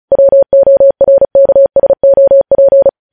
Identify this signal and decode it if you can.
Morse code, heard as a keyed tone, about 36 wpm, 550 Hz. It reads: WORKSOP
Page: News Headlines in Morse Code at 35 WPM